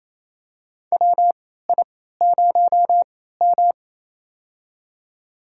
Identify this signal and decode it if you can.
Morse code, heard as a keyed tone, about 28 wpm, spaced as for 16 wpm, 700 Hz. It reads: WI0M
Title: USA Call Signs - 4 Characters - 3x Character Spacing - 28wpm